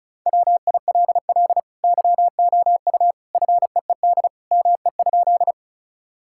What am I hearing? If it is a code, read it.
Morse code, heard as a keyed tone, about 34 wpm, 700 Hz. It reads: WILL YOU FEED ME?
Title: Sentences from Top 1600 Words 35wpm